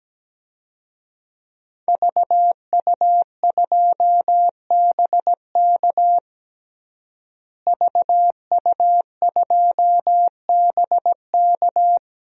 Morse code, heard as a keyed tone, about 17 wpm, 700 Hz. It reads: VU2BK VU2BK
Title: DXCC-100 Call Signs - 5 Characters - Repeated 3x - 17wpm